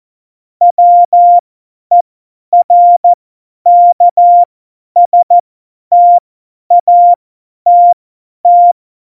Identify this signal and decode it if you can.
Morse code, heard as a keyed tone, about 14 wpm, spaced as for 10 wpm, 700 Hz. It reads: WERKSTATT
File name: Morsekurs / Witze / Witz 26. Februar 2019, 50 F70